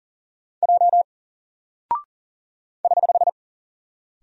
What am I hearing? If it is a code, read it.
Morse code, heard as a keyed tone, about 40 wpm, 700 Hz. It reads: J <HH>
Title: Single Character (Letter:Number:Punctuation:Prosign) Alt 40wpm